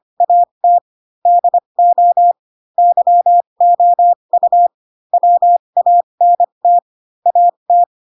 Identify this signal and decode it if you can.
Morse code, heard as a keyed tone, about 25 wpm, spaced as for 22 wpm, 700 Hz. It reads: AT DO YOU WANT AT
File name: Pinocchio / Pinocchio ch6 18 25 ik1hge